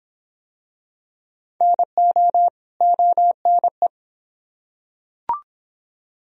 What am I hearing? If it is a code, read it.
Morse code, heard as a keyed tone, about 25 wpm, 700 Hz. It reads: NO ONE